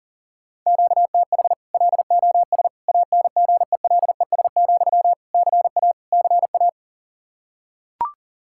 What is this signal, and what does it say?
Morse code, heard as a keyed tone, about 40 wpm, 700 Hz. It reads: QTH LOS ANGELES, CA CA
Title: Practice QSO - Exchange 1 - Beginning - 40wpm